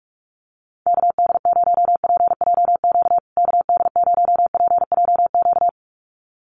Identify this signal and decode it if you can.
Morse code, heard as a keyed tone, about 44 wpm, 700 Hz. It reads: KD0PJQ KD0PJQ
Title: USA Call Signs Repeated 2x 45wpm